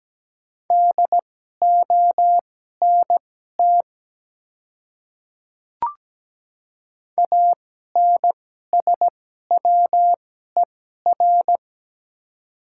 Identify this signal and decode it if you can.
Morse code, heard as a keyed tone, about 17 wpm, spaced as for 12 wpm, 700 Hz. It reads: DONT ANSWER
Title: 2x Character Spacing - Top 300 Words - 17wpm